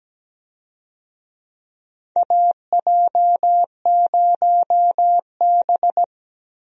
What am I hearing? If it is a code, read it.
Morse code, heard as a keyed tone, about 17 wpm, 700 Hz. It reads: AJ0B
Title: USA Call Signs - 4 Characters - Repeated 3x - 17wpm